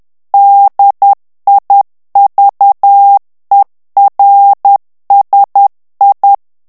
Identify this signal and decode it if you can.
Morse code, heard as a keyed tone, about 10 wpm, 800 Hz. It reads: DIVERSI
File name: ITALIEN texte clair 2 vitesse 10 mots minute